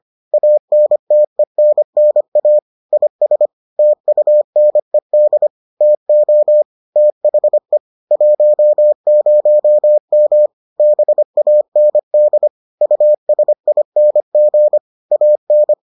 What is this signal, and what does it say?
Morse code, heard as a keyed tone, about 25 wpm, 600 Hz. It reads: ANTENNA IS TUNED TO THE 10M BAND USING AN